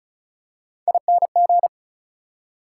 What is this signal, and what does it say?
Morse code, heard as a keyed tone, about 34 wpm, 700 Hz. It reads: ING
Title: Prefixes and Suffixes 35wpm